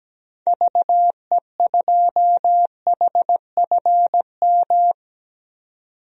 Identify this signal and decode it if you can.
Morse code, heard as a keyed tone, about 17 wpm, 700 Hz. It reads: VE2HFM
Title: DXCC-100 Call Signs - 6 Characters - 17wpm